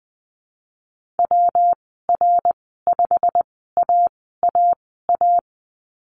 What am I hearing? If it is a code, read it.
Morse code, heard as a keyed tone, about 20 wpm, spaced as for 14 wpm, 700 Hz. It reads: WR5AAA
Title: 193 Lesson 31g - Letter v - Call Signs 20wpm